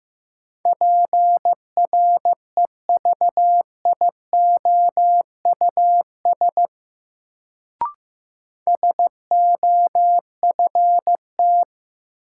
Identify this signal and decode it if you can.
Morse code, heard as a keyed tone, about 15 wpm, 700 Hz. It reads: PREVIOUS SOFT
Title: Sentences from Top 1500 Words - New Word Drill Alt - 15wpm